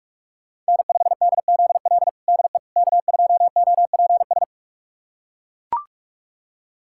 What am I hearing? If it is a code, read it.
Morse code, heard as a keyed tone, about 42 wpm, 700 Hz. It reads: N5DZL DE K2OPS
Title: USA Callsign DE Callsign - 5 Characters - 45wpm